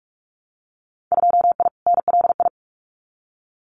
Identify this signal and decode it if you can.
Morse code, heard as a keyed tone, about 44 wpm, 700 Hz. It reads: 2I NLI
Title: Winter Field Day Exchanges 45wpm